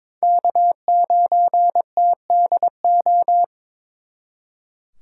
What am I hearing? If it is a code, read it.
Morse code, heard as a keyed tone, about 26 wpm, 700 Hz. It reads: K9TDO
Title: Speed Racing - USA 1x3 Call Signs 22wpm